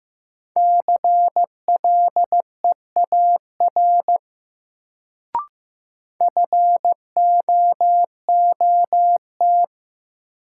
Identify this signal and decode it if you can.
Morse code, heard as a keyed tone, about 15 wpm, 700 Hz. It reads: CLEAR FOOT
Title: Sentences from Top 500 Words - New Word Drill - 15wpm